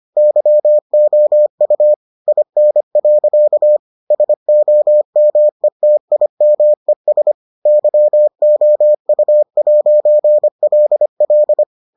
Morse code, heard as a keyed tone, about 25 wpm, 600 Hz. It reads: YOU IN. SOMETIMES YOU'LL